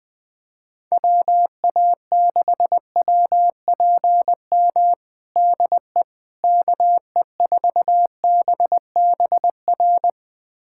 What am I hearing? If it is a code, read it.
Morse code, heard as a keyed tone, about 21 wpm, 700 Hz. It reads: WA6WPM DE KE4BBR
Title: USA Callsign DE Callsign - 6 Characters - 20wpm